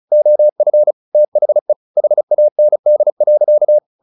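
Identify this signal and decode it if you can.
Morse code, heard as a keyed tone, about 34 wpm, 600 Hz. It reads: OF THE HAND.
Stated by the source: Morse practice at 35 wpm from CW med Gnister, sending Ken Follett's novel Never in English.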